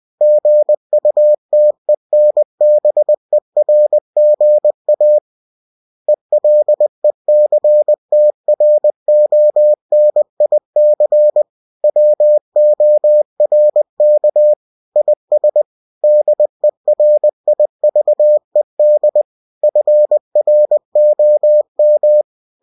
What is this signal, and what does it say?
Morse code, heard as a keyed tone, about 20 wpm, 600 Hz. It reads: GUTENBERGA ELECTRONIC WORK IS DERIVED FROM
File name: cw-books / winnie-the-pooh / winnie-20wpm-page-0115